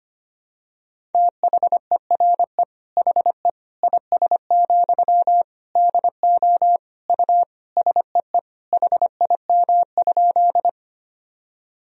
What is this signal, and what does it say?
Morse code, heard as a keyed tone, about 25 wpm, 700 Hz. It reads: THERE HE IS, DO U SEE HIM?
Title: Sentences from Top 100 Words and Abbreviations 25wpm